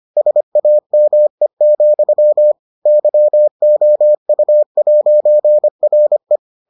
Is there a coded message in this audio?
Yes: Morse code from 0.2 to 6.4 s, heard as a keyed tone, about 25 wpm, 600 Hz. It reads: SAME, YOU'RE